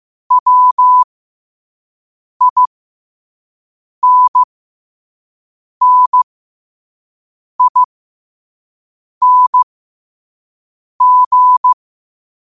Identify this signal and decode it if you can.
Morse code, heard as a keyed tone, about 15 wpm, spaced as for 5 wpm, 1000 Hz. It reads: WINNING
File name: qotdfiles / 5wpm